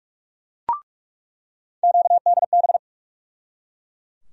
Morse code, heard as a keyed tone, about 46 wpm, 700 Hz. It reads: QDB
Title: USA Call Sign Postfixes - 3 Letters 45wpm